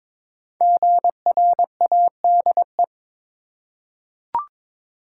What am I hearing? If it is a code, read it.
Morse code, heard as a keyed tone, about 22 wpm, 700 Hz. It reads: GRADE
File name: 5 Letter Words - First 3 Letters Given - 22wpm